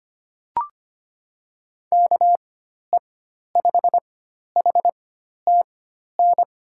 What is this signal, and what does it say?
Morse code, heard as a keyed tone, about 25 wpm, spaced as for 12 wpm, 700 Hz. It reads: KE5HTN
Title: USA Call Signs - 6 Characters - 4x Character Spacing - 25wpm